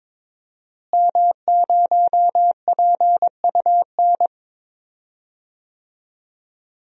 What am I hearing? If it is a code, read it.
Morse code, heard as a keyed tone, about 22 wpm, 700 Hz. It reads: M0PUN
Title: DXCC-100 Call Signs - 5 Characters - Repeated 2x - 22wpm